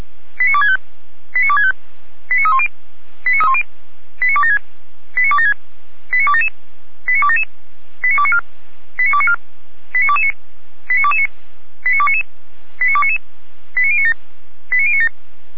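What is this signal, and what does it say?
Five-tone selective call (ZVEI), heard as quick runs of five tones, tones of about 70 ms. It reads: 89265 89265 89310 89276 89276 89277 89277 89294 89294 89229 89229 89299 89299 89097 89097
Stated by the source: Five-tone ZVEI selective call received by SDR, from feuerwehr-orlamuende.org fire brigade audio archive.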